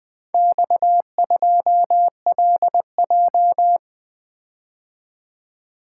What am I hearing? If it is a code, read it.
Morse code, heard as a keyed tone, about 20 wpm, 700 Hz. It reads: X2LJ
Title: Four Letter-Number Combinations - 20wpm